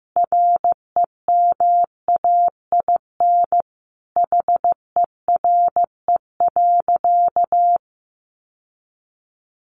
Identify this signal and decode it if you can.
Morse code, heard as a keyed tone, about 15 wpm, 700 Hz. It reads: REMAIN HERE.
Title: Sentences from Top 1200 Words 15wpm